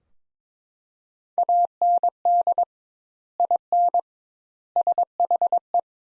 Morse code, heard as a keyed tone, about 22 wpm, 700 Hz. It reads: AND IN SHE
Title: Sentences Top 200 Words - 22wpm